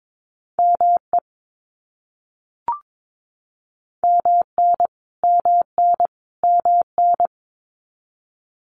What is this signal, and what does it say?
Morse code, heard as a keyed tone, about 22 wpm, 700 Hz. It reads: ME MN MN MN
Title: US States Repeated 3x 22wpm